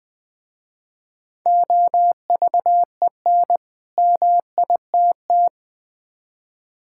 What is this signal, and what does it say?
Morse code, heard as a keyed tone, about 20 wpm, 700 Hz. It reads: OVEN MITT
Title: Easy Words with Hints - 20wpm